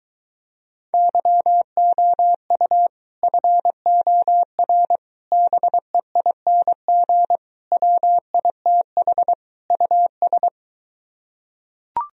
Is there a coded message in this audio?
Yes: Morse code from 0.9 to 10.5 s, heard as a keyed tone, about 23 wpm, 700 Hz. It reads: YOU FOR BEING WITH US